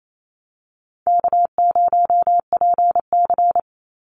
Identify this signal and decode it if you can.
Morse code, heard as a keyed tone, about 28 wpm, 700 Hz. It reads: K0PC
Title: Easy Call Signs Repeated 3x - 28wpm